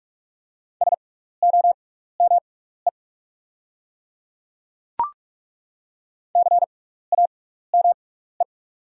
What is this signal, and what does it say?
Morse code, heard as a keyed tone, about 42 wpm, spaced as for 15 wpm, 700 Hz. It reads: SOME CAME
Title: 6x Character Spacing - Top 200 Words - 45wpm